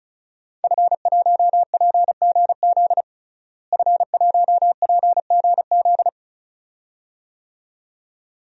Morse code, heard as a keyed tone, about 34 wpm, 700 Hz. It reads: F1PGZ F1PGZ
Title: DXCC-100 Call Signs - 5 Characters - Repeated 2x - 35wpm